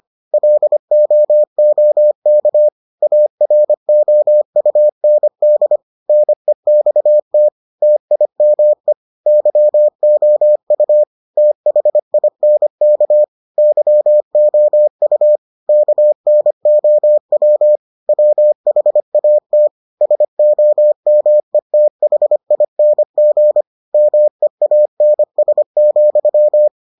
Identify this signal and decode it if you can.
Morse code, heard as a keyed tone, about 25 wpm, 600 Hz. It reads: LOOK AROUND NEXT TIME YOU THINK YOU KNOW WHAT SOMETHING MEANS,